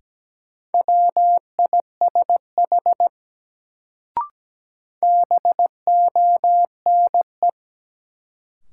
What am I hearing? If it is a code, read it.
Morse code, heard as a keyed tone, about 17 wpm, 700 Hz. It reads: WISH BONE
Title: Sentences from Top 1600 Words - New Word Drill - 17wpm